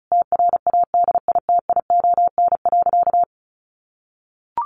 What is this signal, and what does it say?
Morse code, heard as a keyed tone, about 34 wpm, 700 Hz. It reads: TRADITION.
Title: Sentences from Top 1500 Words 35wpm